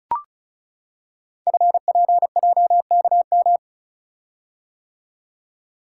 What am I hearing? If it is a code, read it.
Morse code, heard as a keyed tone, about 36 wpm, 700 Hz. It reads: FPJKM
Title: Code Groups - 5 Letter:Numbers - 35wpm